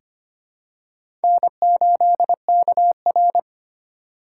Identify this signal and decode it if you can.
Morse code, heard as a keyed tone, about 25 wpm, 700 Hz. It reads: N8KR
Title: USA Call Signs - 4 Characters - 2x Character Spacing - 25wpm